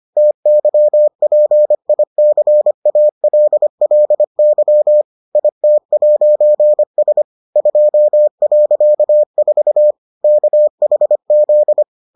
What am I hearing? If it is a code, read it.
Morse code, heard as a keyed tone, about 25 wpm, 600 Hz. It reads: TYPICALLY IT'S 2.4 KHZ